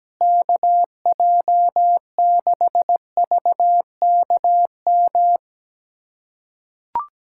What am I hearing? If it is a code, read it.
Morse code, heard as a keyed tone, about 17 wpm, 700 Hz. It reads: KJ6VKM
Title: USA Callsign DE Callsign - 6 Characters - 17wpm